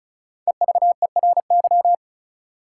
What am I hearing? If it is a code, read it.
Morse code, heard as a keyed tone, about 36 wpm, 700 Hz. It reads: EVERY
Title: Sets of 5 Words - New Word Drill - 35wpm